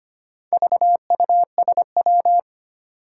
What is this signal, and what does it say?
Morse code, heard as a keyed tone, about 25 wpm, 700 Hz. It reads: VUSW
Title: Four Challenging Letter-Number Combinations 25wpm